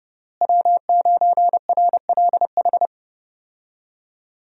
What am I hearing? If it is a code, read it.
Morse code, heard as a keyed tone, about 30 wpm, 700 Hz. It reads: W9RLH
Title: USA Call Signs - 5 Characters - Repeated 2x - 30wpm